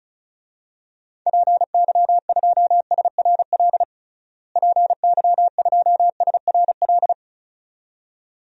Morse code, heard as a keyed tone, about 36 wpm, 700 Hz. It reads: PY2SRL PY2SRL
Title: DXCC-100 Call Signs - 6 Characters - Repeated 2x - 35wpm